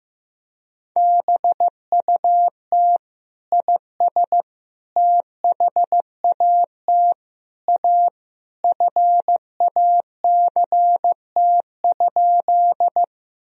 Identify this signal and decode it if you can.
Morse code, heard as a keyed tone, about 16 wpm, 700 Hz. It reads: BUT IS THAT A FACT?
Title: Sentences from Top 1000 Words - Review Part 3 - 15wpm